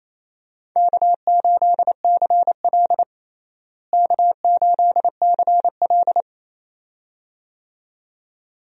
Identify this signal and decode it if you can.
Morse code, heard as a keyed tone, about 28 wpm, 700 Hz. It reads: K8CL K8CL
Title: USA Call Signs - 4 Characters - Repeated 2x - 28wpm